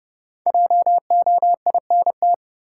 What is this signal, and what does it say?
Morse code, heard as a keyed tone, about 30 wpm, 700 Hz. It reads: JOINT
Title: Sets of 5 Words Encore - New Word Drill - 30wpm